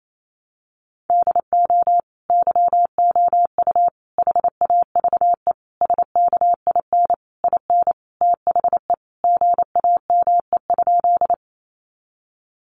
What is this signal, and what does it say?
Morse code, heard as a keyed tone, about 28 wpm, 700 Hz. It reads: DO YOU HAVE SKIN IN THE GAME?
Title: Sentences from Top 1300 Words 28wpm